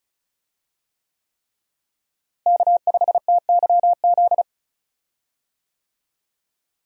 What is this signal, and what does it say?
Morse code, heard as a keyed tone, about 36 wpm, 700 Hz. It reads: K5TYZ
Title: USA Call Signs - 5 Characters - Repeated 3x - 35wpm